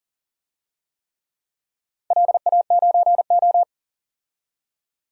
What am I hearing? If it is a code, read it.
Morse code, heard as a keyed tone, about 40 wpm, 700 Hz. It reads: LA9O
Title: DXCC-100 Call Signs - 4 Characters - 40wpm